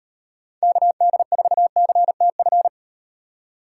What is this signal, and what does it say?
Morse code, heard as a keyed tone, about 38 wpm, 700 Hz. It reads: KD4CTF